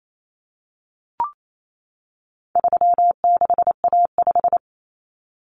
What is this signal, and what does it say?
Morse code, heard as a keyed tone, about 28 wpm, 700 Hz. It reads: 36A5
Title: Common Cut Numbers - 4-Digit 28wpm